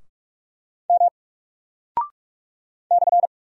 Morse code, heard as a keyed tone, about 44 wpm, 700 Hz. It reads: M /
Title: Single Character (Letter:Number:Punctuation:Prosign) Alt 45wpm